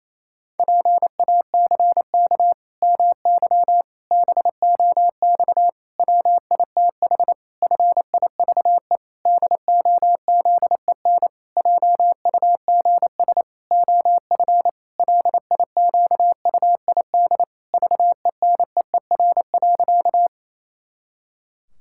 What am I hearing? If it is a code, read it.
Morse code, heard as a keyed tone, about 27 wpm, 700 Hz. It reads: PACK MY BOX WITH FIVE DOZEN JUGS OF LIQUID VENEER.
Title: Pangram Challenge v2 - 28wpm